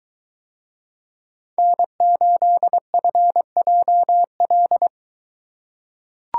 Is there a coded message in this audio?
Yes: Morse code from 1.6 to 4.9 s, heard as a keyed tone, about 23 wpm, 700 Hz. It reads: N8FJL